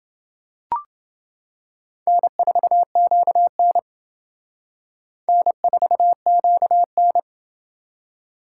Morse code, heard as a keyed tone, about 28 wpm, 700 Hz. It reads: N4QN N4QN